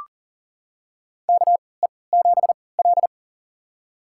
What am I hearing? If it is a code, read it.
Morse code, heard as a keyed tone, about 40 wpm, spaced as for 23 wpm, 700 Hz. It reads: KE7L
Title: USA Call Signs - 4 Characters - 3x Character Spacing - 40wpm